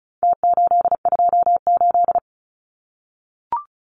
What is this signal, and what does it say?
Morse code, heard as a keyed tone, about 34 wpm, 700 Hz. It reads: T828
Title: Numbers - Contest Style - 4-Digit 35wpm